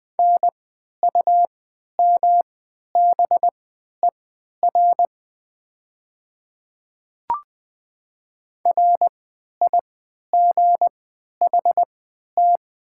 Morse code, heard as a keyed tone, about 21 wpm, spaced as for 11 wpm, 700 Hz. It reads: NUMBER RIGHT
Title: 3x Character Spacing - Top 500 Words - 20wpm